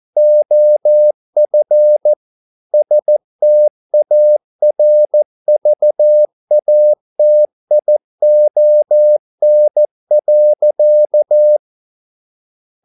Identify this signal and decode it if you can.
Morse code, heard as a keyed tone, about 14 wpm, 600 Hz. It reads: OF STARVATION.